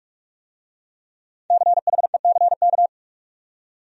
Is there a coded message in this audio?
Yes: Morse code from 1.5 to 2.9 s, heard as a keyed tone, about 41 wpm, 700 Hz. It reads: CHECK